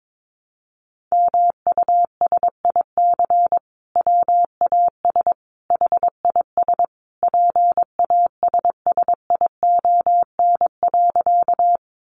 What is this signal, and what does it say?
Morse code, heard as a keyed tone, about 22 wpm, 700 Hz. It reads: MUSIC WAS HIS PASSION.
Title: Sentences from Top 1300 Words 22wpm